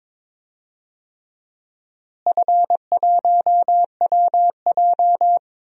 Morse code, heard as a keyed tone, about 22 wpm, 700 Hz. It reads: F1WJ
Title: DXCC-100 Call Signs - 4 Characters - Repeated 3x - 22wpm